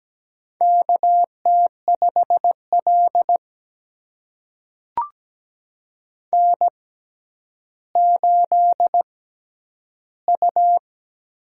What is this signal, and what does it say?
Morse code, heard as a keyed tone, about 17 wpm, spaced as for 10 wpm, 700 Hz. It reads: KT5L N8U
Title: USA Call Signs - 6x Character Spacing - 17wpm